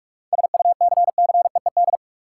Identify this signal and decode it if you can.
Morse code, heard as a keyed tone, about 44 wpm, 700 Hz. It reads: SUCCEED
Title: Seven Letter Words - 45wpm